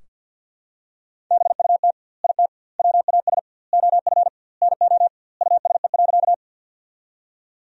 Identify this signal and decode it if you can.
Morse code, heard as a keyed tone, about 46 wpm, 700 Hz. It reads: BUT IT WAS OF NO USE.
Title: 2x Word-Spacing - Sentences from Top 100 Words - 50wpm